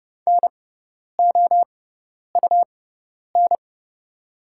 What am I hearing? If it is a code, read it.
Morse code, heard as a keyed tone, about 30 wpm, spaced as for 10 wpm, 700 Hz. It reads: NOUN
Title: 6x Character Spacing - Top 500 Words - 30wpm